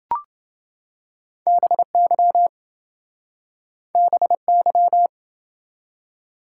Morse code, heard as a keyed tone, about 29 wpm, 700 Hz. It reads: BY BY